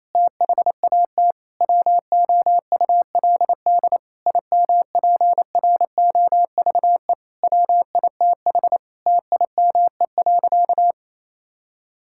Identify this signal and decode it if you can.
Morse code, heard as a keyed tone, about 28 wpm, 700 Hz. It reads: THAT WOULD IMPROVE WITH TIME.
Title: Sentences from Top 1000 Words - Review Part 5 - 28wpm